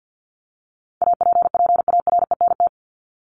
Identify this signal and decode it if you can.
Morse code, heard as a keyed tone, about 50 wpm, 700 Hz. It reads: APPARENT
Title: LY Suffix - Top 100 Words with Base Words 50wpm